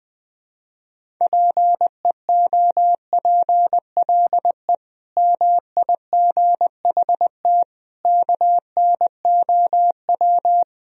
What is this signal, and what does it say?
Morse code, heard as a keyed tone, about 20 wpm, 700 Hz. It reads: PEOPLE MIGHT KNOW
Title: Sentences from Top 500 Words and Abbreviations 20wpm